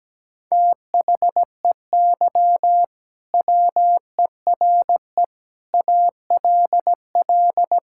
Morse code, heard as a keyed tone, about 17 wpm, 700 Hz. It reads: THEY WERE ALL
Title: Sentences from Top 1000 Words - Review Part 5 - 17wpm